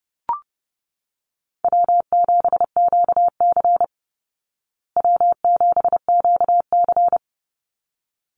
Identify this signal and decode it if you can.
Morse code, heard as a keyed tone, about 30 wpm, 700 Hz. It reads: W7QC W7QC
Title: Call Signs Repeated 3x - 30wpm